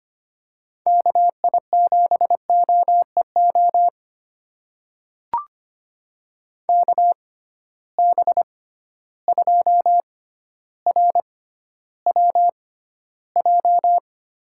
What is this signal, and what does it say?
Morse code, heard as a keyed tone, about 25 wpm, spaced as for 13 wpm, 700 Hz. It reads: KI7OEO KB2RWJ
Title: USA Call Signs - 6 Characters - 6x Character Spacing - 25wpm